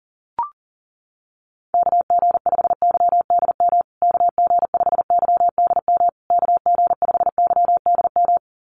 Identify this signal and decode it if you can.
Morse code, heard as a keyed tone, about 40 wpm, 700 Hz. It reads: KG5YDM KG5YDM KG5YDM
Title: USA Call Signs Repeated 3x 40wpm